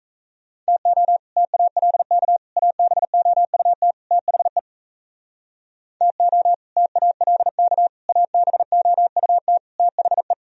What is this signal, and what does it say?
Morse code, heard as a keyed tone, about 36 wpm, 700 Hz. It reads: TO TALK ABOUT THE TO TALK ABOUT THE